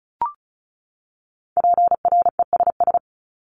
Morse code, heard as a keyed tone, about 34 wpm, 700 Hz. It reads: PRESS
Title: Top 500 Words 35wpm